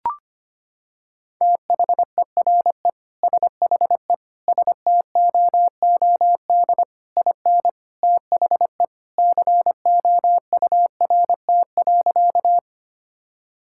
Morse code, heard as a keyed tone, about 25 wpm, 700 Hz. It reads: THERE SHE STOOD IN THE COURT.
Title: Sentences from Top 1600 Words 25wpm